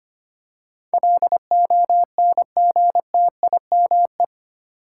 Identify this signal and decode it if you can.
Morse code, heard as a keyed tone, about 25 wpm, 700 Hz. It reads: LONGTIME
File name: 8 Letter Words - First 3 Letters Given - 25wpm